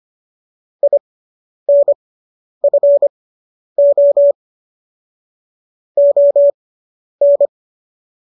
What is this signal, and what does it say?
Morse code, heard as a keyed tone, about 25 wpm, spaced as for 10 wpm, 570 Hz. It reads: INFO ON